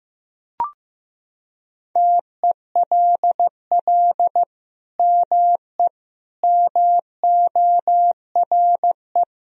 Morse code, heard as a keyed tone, about 15 wpm, 700 Hz. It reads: TELL ME MORE
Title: Sentences from Top 1000 Words - Review Part 4 - 15wpm